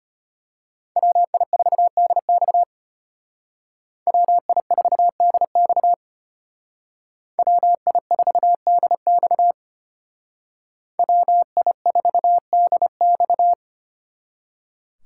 Morse code, heard as a keyed tone, about 32 wpm, 700 Hz. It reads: WI4DX WI4DX WI4DX WI4DX